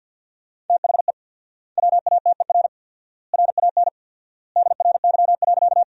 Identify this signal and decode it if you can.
Morse code, heard as a keyed tone, about 50 wpm, 700 Hz. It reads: THE WATER RAN DRY.
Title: 4x Word-Spacing - Sentences from Top 500 Words - 50wpm